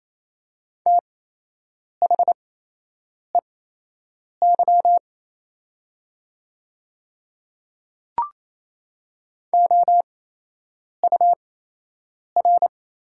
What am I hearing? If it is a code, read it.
Morse code, heard as a keyed tone, about 28 wpm, spaced as for 8 wpm, 700 Hz. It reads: THEY OUR
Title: 8x Character Spacing - Top 200 Words - 28wpm